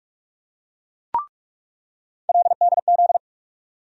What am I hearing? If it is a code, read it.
Morse code, heard as a keyed tone, about 45 wpm, 700 Hz. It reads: LDZ
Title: USA Call Sign Postfixes - 3 Letters 45wpm